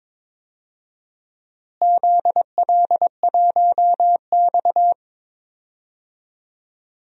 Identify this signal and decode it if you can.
Morse code, heard as a keyed tone, about 21 wpm, 700 Hz. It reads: ZL1X
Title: DXCC-100 Call Signs - 4 Characters - Repeated 3x - 22wpm